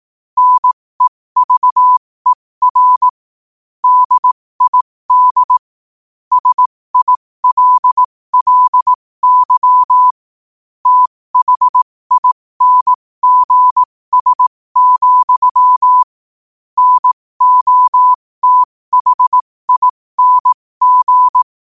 Morse code, heard as a keyed tone, about 18 wpm, spaced as for 15 wpm, 1000 Hz. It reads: NEVER DID SILLY THINGS, NOTHING
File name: qotdfiles / 15wpm